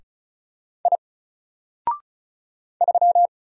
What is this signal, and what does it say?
Morse code, heard as a keyed tone, about 34 wpm, 700 Hz. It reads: I 3
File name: Single-Letter-Number-35wpm